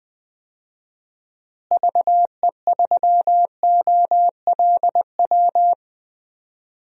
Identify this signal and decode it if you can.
Morse code, heard as a keyed tone, about 20 wpm, 700 Hz. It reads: VE3OLW
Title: DXCC-100 Call Signs - 6 Characters - Repeated 2x - 20wpm